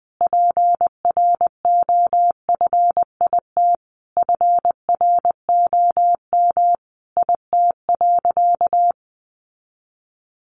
Morse code, heard as a keyed tone, about 20 wpm, 700 Hz. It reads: PROFIT FROM IT.
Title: Sentences from Top 800 Words 20wpm